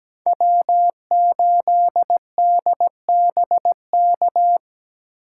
Morse code, heard as a keyed tone, about 17 wpm, 700 Hz. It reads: W8DBK
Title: USA Call Signs - 5 Characters - 6x Character Spacing - 17wpm